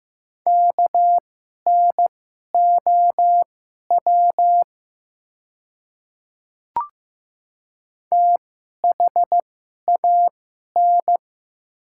Morse code, heard as a keyed tone, about 15 wpm, spaced as for 11 wpm, 700 Hz. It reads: KNOW THAN
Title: 2x Character Spacing - Top 100 Words - 15wpm